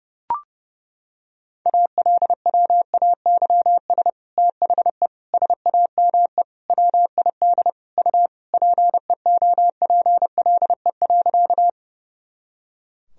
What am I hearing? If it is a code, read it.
Morse code, heard as a keyed tone, about 32 wpm, 700 Hz. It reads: ALWAYS THE SAME WID U PEOPLE.
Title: Sentences from Top 300 Words and Abbreviations 30wpm